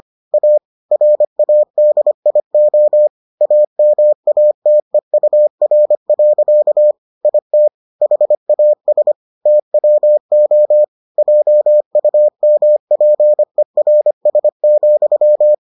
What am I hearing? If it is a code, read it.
Morse code, heard as a keyed tone, about 25 wpm, 600 Hz. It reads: A RADIO AMATEUR. IT HAS TWO JUMPERS,